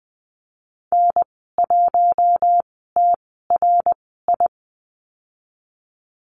Morse code, heard as a keyed tone, about 19 wpm, spaced as for 14 wpm, 700 Hz. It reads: N1TRI
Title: 088 Lesson 15g - Number 5 - Call Signs 20wpm